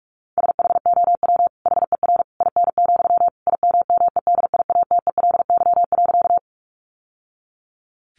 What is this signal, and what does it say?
Morse code, heard as a keyed tone, about 44 wpm, 700 Hz. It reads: SHOW HER IN, IMMEDIATELY.
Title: Sentences from Top 1100 Words 45wpm